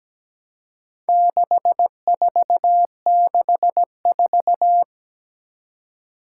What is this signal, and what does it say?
Morse code, heard as a keyed tone, about 17 wpm, 700 Hz. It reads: 6464
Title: Four Challenging Letter-Number Combinations 17wpm